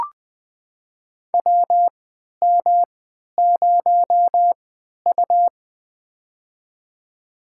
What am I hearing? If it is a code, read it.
Morse code, heard as a keyed tone, about 20 wpm, spaced as for 11 wpm, 700 Hz. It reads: WM0U
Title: USA Call Signs - 4 Characters - 3x Character Spacing - 20wpm